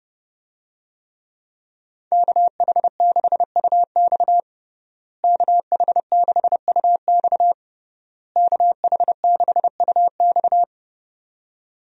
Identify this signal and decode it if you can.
Morse code, heard as a keyed tone, about 30 wpm, 700 Hz. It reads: KH6UX KH6UX KH6UX
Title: USA Call Signs - 5 Characters - Repeated 3x - 30wpm